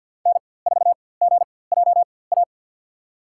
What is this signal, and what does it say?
Morse code, heard as a keyed tone, about 52 wpm, spaced as for 23 wpm, 690 Hz. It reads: N4GJA
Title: USA Call Signs - 5 Characters - 4x Character Spacing - 50wpm